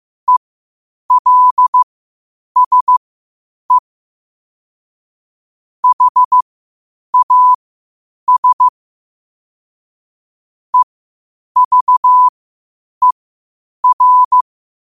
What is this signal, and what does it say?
Morse code, heard as a keyed tone, about 15 wpm, spaced as for 8 wpm, 1000 Hz. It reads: ELSE HAS EVER
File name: 8wpm